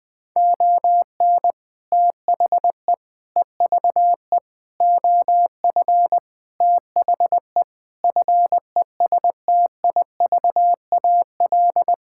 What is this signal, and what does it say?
Morse code, heard as a keyed tone, about 20 wpm, 700 Hz. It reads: ON THE EVE OF THE FESTIVAL
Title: Sentences with 5-Word Phrases - Spoken Beginning - 20wpm